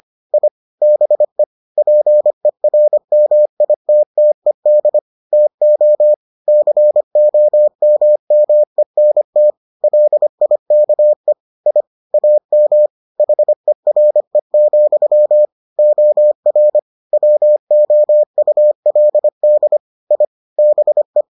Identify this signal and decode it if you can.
Morse code, heard as a keyed tone, about 25 wpm, 600 Hz. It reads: I BE PERMITTED TO COMMENT LIKE I AM HERE, OR WOULD I BE